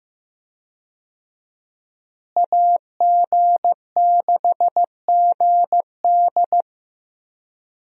Morse code, heard as a keyed tone, about 15 wpm, 700 Hz. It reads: AG6GD
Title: USA Call Signs - 5 Characters - Repeated 2x - 15wpm